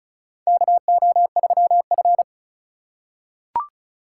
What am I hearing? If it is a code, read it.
Morse code, heard as a keyed tone, about 36 wpm, 700 Hz. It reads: KO3F
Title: USA Call Signs - 4 Characters - 4x Character Spacing - 35wpm